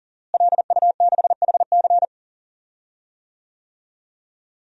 Morse code, heard as a keyed tone, about 40 wpm, 700 Hz. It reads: LU6HC
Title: DXCC-100 Call Signs - 5 Characters - Repeated 2x - 40wpm